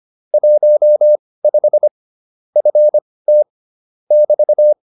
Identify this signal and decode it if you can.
Morse code, heard as a keyed tone, about 25 wpm, spaced as for 18 wpm, 600 Hz. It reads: 15 FT <BT>
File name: QSO302 18